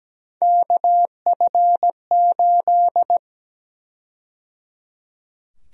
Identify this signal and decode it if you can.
Morse code, heard as a keyed tone, about 17 wpm, 700 Hz. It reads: KF8
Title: USA Call Sign Prefixes - 17wpm